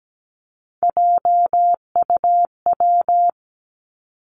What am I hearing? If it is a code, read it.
Morse code, heard as a keyed tone, about 17 wpm, 700 Hz. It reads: JUW
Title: Three Challenging Letter-Number Combinations 17wpm